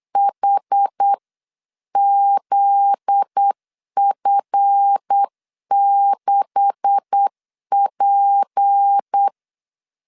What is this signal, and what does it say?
Morse code, heard as a keyed tone, about 8 wpm, 800 Hz. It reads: HZF6P
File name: LEZIONI / 9 286 8wpm 3425